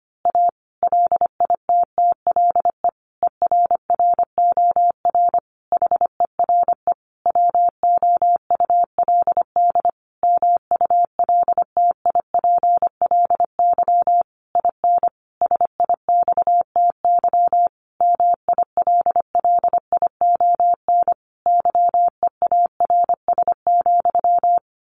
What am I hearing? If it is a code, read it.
Morse code, heard as a keyed tone, about 25 wpm, 700 Hz. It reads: A LITTLE ERROR HERE WOULD MULTIPLY IN SIXTY MILLION YEARS,
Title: A Sound Of Thunder 25wpm